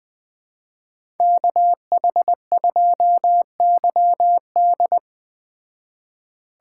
Morse code, heard as a keyed tone, about 20 wpm, 700 Hz. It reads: KH2YD
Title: Speed Racing - USA 2x2 Call Signs 17wpm